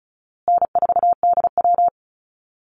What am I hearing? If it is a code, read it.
Morse code, heard as a keyed tone, about 34 wpm, 700 Hz. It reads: N4DW
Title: Easy Call Signs Repeated 3x - 35wpm